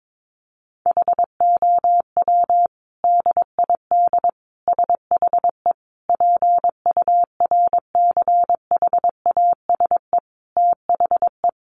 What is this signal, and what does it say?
Morse code, heard as a keyed tone, about 22 wpm, 700 Hz. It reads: HOW DID SHE PURCHASE THE
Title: Sentences from Top 800 Words 22wpm